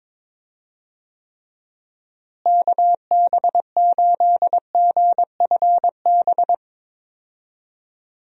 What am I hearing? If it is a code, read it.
Morse code, heard as a keyed tone, about 21 wpm, 700 Hz. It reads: KB8GFB
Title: USA Call Signs - 6 Characters - Repeated 2x - 22wpm